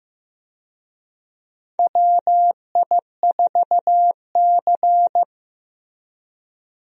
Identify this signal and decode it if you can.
Morse code, heard as a keyed tone, about 15 wpm, 700 Hz. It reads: WI4C
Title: USA Call Signs - 4 Characters - Repeated 3x - 15wpm